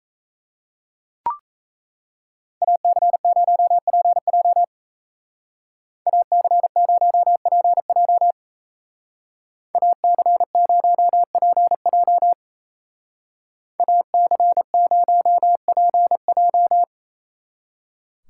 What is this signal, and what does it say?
Morse code, heard as a keyed tone, about 35 wpm, 700 Hz. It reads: AC0PJ AC0PJ AC0PJ AC0PJ
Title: Speed Racing - USA Call Signs 28wpm